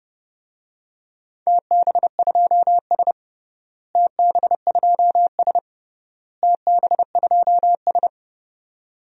Morse code, heard as a keyed tone, about 31 wpm, 700 Hz. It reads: TB2S TB2S TB2S
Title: DXCC-100 Call Signs - 4 Characters - Repeated 3x - 30wpm